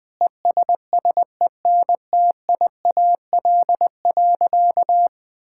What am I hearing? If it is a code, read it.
Morse code, heard as a keyed tone, about 20 wpm, 700 Hz. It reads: ESSENTIAL.
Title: Sentences with 8-Word Phrases - 20wpm